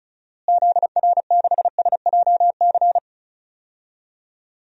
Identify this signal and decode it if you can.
Morse code, heard as a keyed tone, about 36 wpm, 700 Hz. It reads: ZR6SJC
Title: DXCC-100 Call Signs - 6 Characters - Repeated 3x - 35wpm